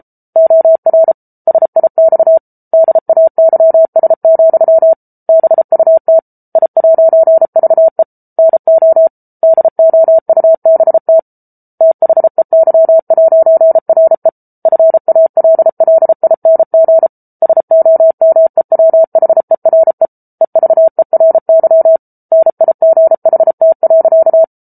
Morse code, heard as a keyed tone, about 34 wpm, 650 Hz. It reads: OR SIX DAYS, BUT I'VE NO DOUBT THEY'RE FALLING SOMEWHERE EVERY NIGHT.